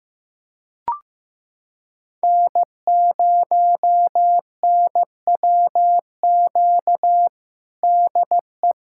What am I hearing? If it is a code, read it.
Morse code, heard as a keyed tone, about 15 wpm, 700 Hz. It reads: N0NWQ DE
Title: USA Callsign DE Callsign - 5 Characters - 15wpm